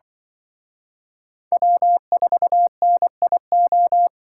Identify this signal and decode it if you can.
Morse code, heard as a keyed tone, about 24 wpm, 700 Hz. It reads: W4NIO
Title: Speed Racing - USA Call Signs 20wpm